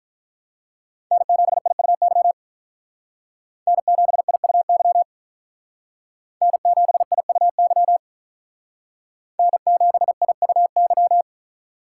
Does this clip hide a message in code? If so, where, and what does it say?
Yes: Morse code from 1.1 to 11.2 s, heard as a keyed tone, about 45 wpm, 700 Hz. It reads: N7IUY N7IUY N7IUY N7IUY